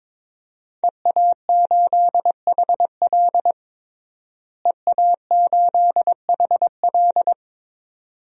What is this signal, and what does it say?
Morse code, heard as a keyed tone, about 21 wpm, 700 Hz. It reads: EA8HL EA8HL
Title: DXCC-100 Call Signs - 5 Characters - Repeated 2x - 22wpm